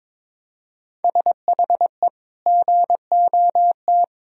Morse code, heard as a keyed tone, about 23 wpm, 700 Hz. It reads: SHE GOT
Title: Sentences from Top 500 Words and Abbreviations 22wpm